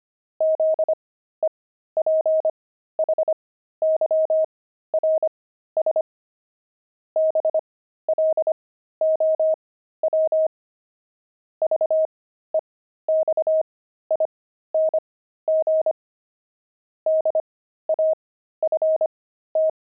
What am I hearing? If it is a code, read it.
Morse code, heard as a keyed tone, about 25 wpm, spaced as for 13 wpm, 630 Hz. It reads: ZEPHYRS BLOW VEXING DAFT
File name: pangram-fs-13-cs-25-zebras